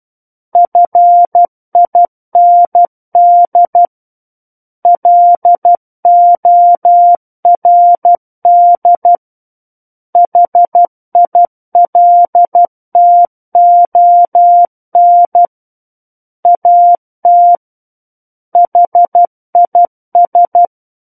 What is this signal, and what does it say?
Morse code, heard as a keyed tone, about 12 wpm, 700 Hz. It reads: FIND LORD HILTON AT HIS
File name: Book One / 03-Chapter 3 - 12 WPM